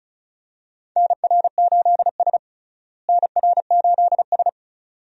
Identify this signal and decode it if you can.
Morse code, heard as a keyed tone, about 34 wpm, 700 Hz. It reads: NR8S NR8S
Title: USA Call Signs - 4 Characters - Repeated 2x - 35wpm